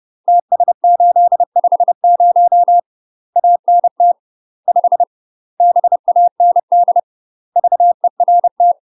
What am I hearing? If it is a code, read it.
Morse code, heard as a keyed tone, about 30 wpm, 700 Hz. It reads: TS850 ANT 5 BAND VERT
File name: UK MASTER QSOs TRAINING / 30WPM / UK MASTER QSOs 0039